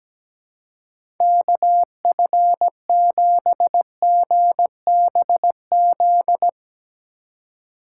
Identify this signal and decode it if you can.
Morse code, heard as a keyed tone, about 17 wpm, 700 Hz. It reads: KF7GBZ
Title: USA Call Signs - 6 Characters - 4x Character Spacing - 17wpm